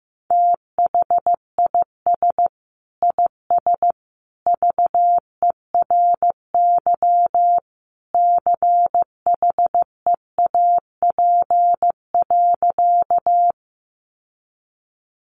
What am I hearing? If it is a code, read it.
Morse code, heard as a keyed tone, about 15 wpm, 700 Hz. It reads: THIS IS VERY CHEAP.
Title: Sentences from Top 1200 Words 15wpm